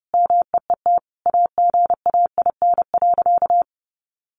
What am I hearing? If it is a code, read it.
Morse code, heard as a keyed tone, about 30 wpm, 700 Hz. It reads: MEET AGAIN.
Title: Sentences from Top 1100 Words 30wpm